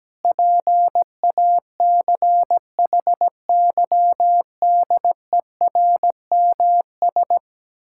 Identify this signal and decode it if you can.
Morse code, heard as a keyed tone, about 17 wpm, 700 Hz. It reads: PACHYDERMS
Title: Pangram Challenge v2 - 17wpm